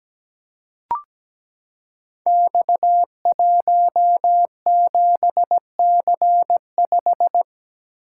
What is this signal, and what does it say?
Morse code, heard as a keyed tone, about 17 wpm, 700 Hz. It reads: X17C5
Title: Code Groups - 5 Letter:Numbers - 17wpm